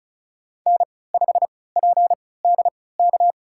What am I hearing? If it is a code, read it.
Morse code, heard as a keyed tone, about 34 wpm, spaced as for 20 wpm, 700 Hz. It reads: N5PDK
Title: USA Call Signs - 5 Characters - 3x Character Spacing - 35wpm